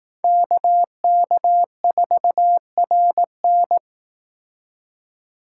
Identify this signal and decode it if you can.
Morse code, heard as a keyed tone, about 18 wpm, 700 Hz. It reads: KK4RN
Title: Speed Racing - USA 2x2 Call Signs 15wpm